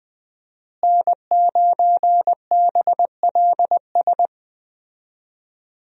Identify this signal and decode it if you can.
Morse code, heard as a keyed tone, about 20 wpm, 700 Hz. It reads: N9BLS
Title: USA Call Signs - 5 Characters - Repeated 2x - 20wpm